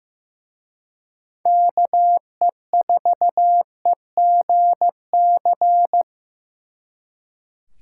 Morse code, heard as a keyed tone, about 17 wpm, 700 Hz. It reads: KE4EGC
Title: Speed Racing - USA 2x3 Call Signs 15wpm